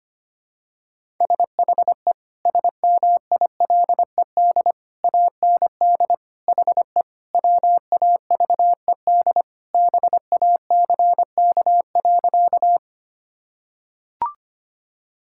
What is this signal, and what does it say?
Morse code, heard as a keyed tone, about 25 wpm, 700 Hz. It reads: SHE SMILED AND HE WAVED BACK.
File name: practice / 25wpm / Sentences with 2-Word Phrases - 25wpm